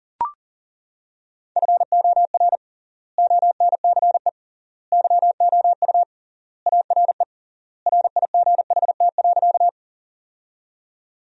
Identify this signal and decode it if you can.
Morse code, heard as a keyed tone, about 40 wpm, 700 Hz. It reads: FOR ONCE YOU ARE RIGHT.
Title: 3x Word-Spacing - Sentences from Top 300 Words - 40wpm